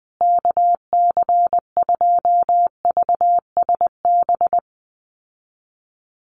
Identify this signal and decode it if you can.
Morse code, heard as a keyed tone, about 19 wpm, 700 Hz. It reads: KC2VSB
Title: USA Call Signs Repeated 3x 20wpm